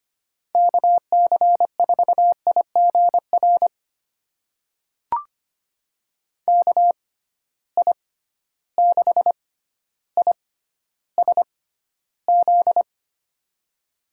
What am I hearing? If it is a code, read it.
Morse code, heard as a keyed tone, about 25 wpm, spaced as for 13 wpm, 700 Hz. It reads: KC4IGR KI6ISZ
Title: USA Call Signs - 6 Characters - 6x Character Spacing - 25wpm